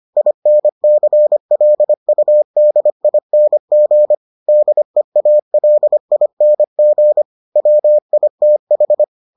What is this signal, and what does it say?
Morse code, heard as a keyed tone, about 25 wpm, 600 Hz. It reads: INCLUDING DEALING WITH